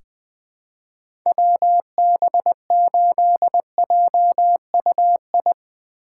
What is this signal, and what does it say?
Morse code, heard as a keyed tone, about 20 wpm, 700 Hz. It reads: WB8JUI
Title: K1USN SST QSO Practice - 20wpm